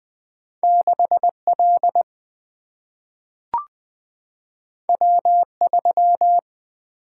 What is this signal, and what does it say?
Morse code, heard as a keyed tone, about 20 wpm, 700 Hz. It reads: 6 L W 3
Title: Single-Letter Single-Number 20wpm